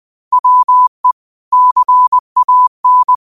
Morse code, heard as a keyed tone, about 20 wpm, 1000 Hz. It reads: WE CAN
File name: qotdfiles / 20wpm